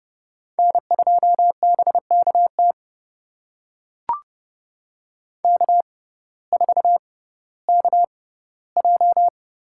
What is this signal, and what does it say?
Morse code, heard as a keyed tone, about 30 wpm, spaced as for 16 wpm, 700 Hz. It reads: N2BKT K4KJ
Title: USA Call Signs - 5 Characters - 6x Character Spacing - 30wpm